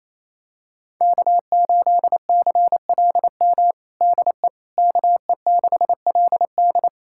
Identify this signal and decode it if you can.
Morse code, heard as a keyed tone, about 28 wpm, 700 Hz. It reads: K8CLM DE KE6LD
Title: USA Callsign DE Callsign - 5 Characters - 28wpm